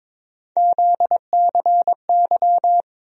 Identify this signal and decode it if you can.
Morse code, heard as a keyed tone, about 21 wpm, 700 Hz. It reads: ZCY
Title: USA Call Sign Postfixes - 3 Letters 22wpm